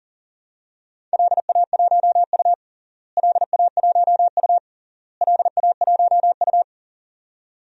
Morse code, heard as a keyed tone, about 40 wpm, 700 Hz. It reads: LA1U LA1U LA1U
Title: DXCC-100 Call Signs - 4 Characters - Repeated 3x - 40wpm